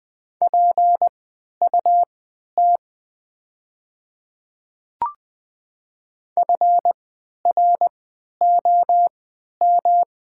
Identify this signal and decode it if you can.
Morse code, heard as a keyed tone, about 20 wpm, spaced as for 11 wpm, 690 Hz. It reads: PUT FROM
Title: 3x Character Spacing - Top 500 Words - 20wpm